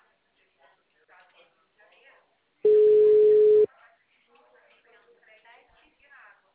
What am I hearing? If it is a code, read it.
Ringback tone (Europe, 425 Hz), with speech.
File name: out- -2012- - - .44582